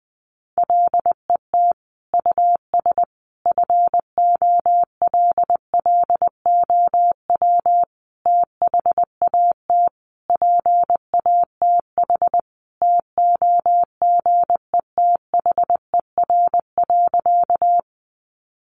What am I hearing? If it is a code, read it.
Morse code, heard as a keyed tone, about 20 wpm, 700 Hz. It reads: LET US FOLLOW THAT PATH TOGETHER.
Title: Sentences from Top 1400 Words 20wpm